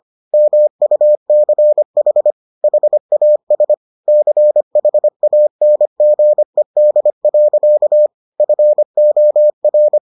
Morse code, heard as a keyed tone, about 25 wpm, 600 Hz. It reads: MUCH HAS CHANGED. FOR